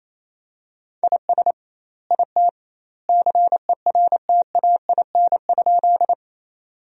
Morse code, heard as a keyed tone, about 28 wpm, 700 Hz. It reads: IS IT CERTAIN?
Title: 2x Word-Spacing - Sentences from Top 500 Words - 28wpm